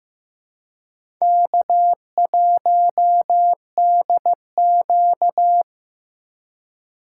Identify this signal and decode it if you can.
Morse code, heard as a keyed tone, about 15 wpm, 700 Hz. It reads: K1DQ
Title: USA Call Signs - 4 Characters - 3x Character Spacing - 15wpm